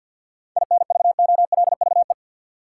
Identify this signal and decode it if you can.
Morse code, heard as a keyed tone, about 53 wpm, 700 Hz. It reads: INVOLVE
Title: Sentences from Top 1200 Words - New Word Drill - 50wpm